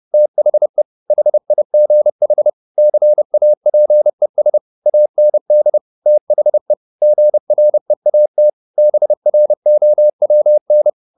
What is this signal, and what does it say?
Morse code, heard as a keyed tone, about 30 wpm, 600 Hz. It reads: THE HIGH CAPES AND THE GREAT BROWN
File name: the old man and the sea / the old man and the sea 1 30wpm